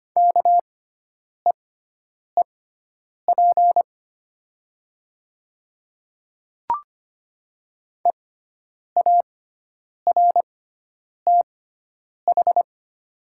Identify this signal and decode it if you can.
Morse code, heard as a keyed tone, about 25 wpm, spaced as for 9 wpm, 700 Hz. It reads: KEEP EARTH
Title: 6x Character Spacing - Top 500 Words - 25wpm